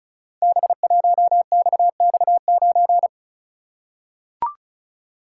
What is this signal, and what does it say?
Morse code, heard as a keyed tone, about 36 wpm, 690 Hz. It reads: B1XX9
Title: Code Groups - 5 Letter:Numbers - 35wpm